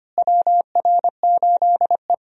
Morse code, heard as a keyed tone, about 25 wpm, 700 Hz. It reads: WR8E
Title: USA Call Signs - 4 Characters - Repeated 3x - 25wpm